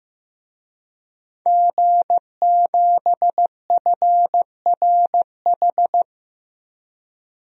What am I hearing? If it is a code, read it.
Morse code, heard as a keyed tone, about 15 wpm, 700 Hz. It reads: G7FRH
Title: DXCC-100 Call Signs - 5 Characters - 15wpm